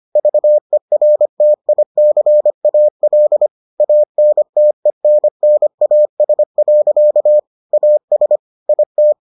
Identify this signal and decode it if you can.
Morse code, heard as a keyed tone, about 25 wpm, 600 Hz. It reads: VERTICAL ANTENNAS. AS IT